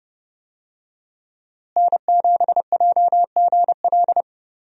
Morse code, heard as a keyed tone, about 30 wpm, 700 Hz. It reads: N7JGL
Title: USA Call Signs - 5 Characters - Repeated 3x - 30wpm